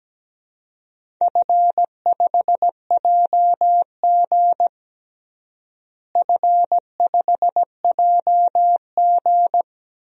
Morse code, heard as a keyed tone, about 17 wpm, 700 Hz. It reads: F5JG F5JG
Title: DXCC-100 Call Signs - 4 Characters - Repeated 2x - 17wpm